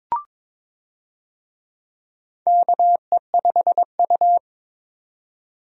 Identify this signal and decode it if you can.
Morse code, heard as a keyed tone, about 22 wpm, 700 Hz. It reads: KE5U
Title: USA Call Signs - 4 Characters - 22wpm